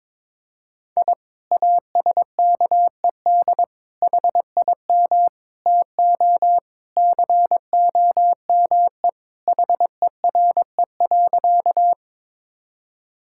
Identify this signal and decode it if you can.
Morse code, heard as a keyed tone, about 22 wpm, 700 Hz. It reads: I ASKED HIM TO COME HERE.
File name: practice / 22wpm / Sentences with 2-Word Phrases - 22wpm